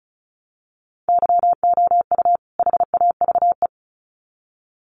Morse code, heard as a keyed tone, about 34 wpm, 700 Hz. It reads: YOU HAVE
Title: Sets of 2 Words 35wpm